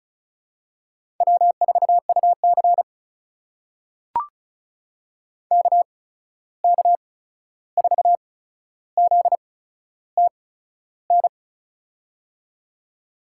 Morse code, heard as a keyed tone, about 36 wpm, spaced as for 13 wpm, 700 Hz. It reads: W4UC KK4ZTN
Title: USA Call Signs - 8x Character Spacing - 35wpm